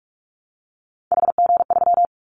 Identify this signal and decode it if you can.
Morse code, heard as a keyed tone, about 46 wpm, 700 Hz. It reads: HG3
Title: Callsign Prefixes for Top 50 DXCC - 45wpm